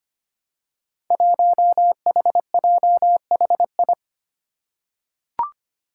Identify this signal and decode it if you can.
Morse code, heard as a keyed tone, about 25 wpm, 700 Hz. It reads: 1HJHI
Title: Five Challenging Letter-Number Combinations 25wpm